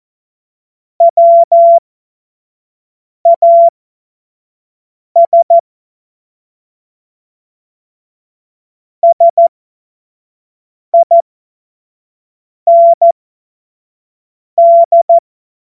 Morse code, heard as a keyed tone, about 14 wpm, spaced as for 5 wpm, 680 Hz. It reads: WAS SIND
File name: Morsekurs / Witze / Witz 28.November 2023 25 F70